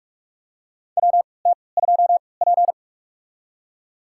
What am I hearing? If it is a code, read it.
Morse code, heard as a keyed tone, about 46 wpm, spaced as for 26 wpm, 690 Hz. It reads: WT2P
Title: USA Call Signs - 4 Characters - 3x Character Spacing - 45wpm